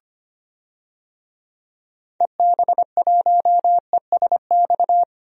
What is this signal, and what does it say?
Morse code, heard as a keyed tone, about 25 wpm, 700 Hz. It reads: EB1ESX
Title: DXCC-100 Call Signs - 6 Characters - 25wpm